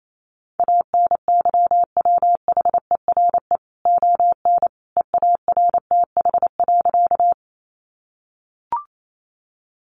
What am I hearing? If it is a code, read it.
Morse code, heard as a keyed tone, about 28 wpm, 700 Hz. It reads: ANYWHERE ON EARTH.
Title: Sentences from Top 1400 Words 28wpm